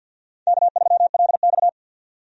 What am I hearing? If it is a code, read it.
Morse code, heard as a keyed tone, about 52 wpm, 700 Hz. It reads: K3LX
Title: USA Call Signs - 4 Characters - 50wpm